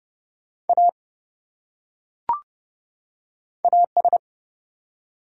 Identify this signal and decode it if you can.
Morse code, heard as a keyed tone, about 30 wpm, 700 Hz. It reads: A AS
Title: Top 100 Words 30wpm